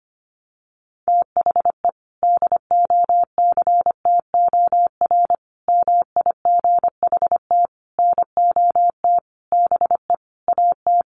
Morse code, heard as a keyed tone, about 25 wpm, 700 Hz. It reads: THE DOCTOR MIGHT NOT BE AT
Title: Sentences from Top 1500 Words 25wpm